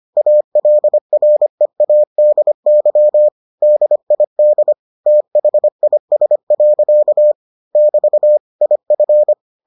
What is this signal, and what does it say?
Morse code, heard as a keyed tone, about 25 wpm, 600 Hz. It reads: ALREADY DID THIS. <BT> IF